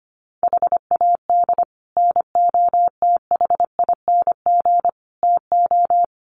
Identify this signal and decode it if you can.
Morse code, heard as a keyed tone, about 25 wpm, 700 Hz. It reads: HAD NOTHING TO
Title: Sets of 4 Words 25wpm